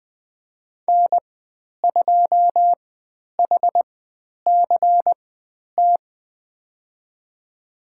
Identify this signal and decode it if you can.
Morse code, heard as a keyed tone, about 20 wpm, spaced as for 10 wpm, 700 Hz. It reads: N2HCT
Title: 102 Lesson 17g - Letter u - Call Signs 20wpm